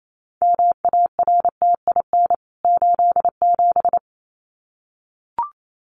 Morse code, heard as a keyed tone, about 28 wpm, 700 Hz. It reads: MARTIN 87
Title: CWT Exchange 28wpm